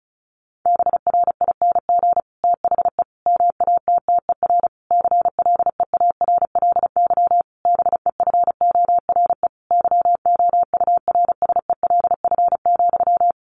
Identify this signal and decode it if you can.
Morse code, heard as a keyed tone, about 34 wpm, 700 Hz. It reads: BRING THE MATTER CLEARLY BEFORE YOURSELF,
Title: Sentences from Top 1300 Words 35wpm